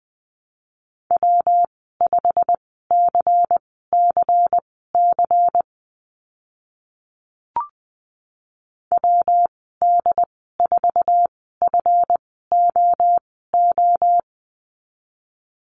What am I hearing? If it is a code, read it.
Morse code, heard as a keyed tone, about 19 wpm, spaced as for 14 wpm, 700 Hz. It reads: W5CCC WD4FOO
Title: 140 Lesson 23g - Letter f - Call Signs 20wpm